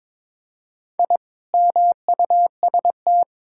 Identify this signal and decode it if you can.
Morse code, heard as a keyed tone, about 22 wpm, 700 Hz. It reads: I MUST
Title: Sentences from Top 1000 Words - Review Part 6 - 22wpm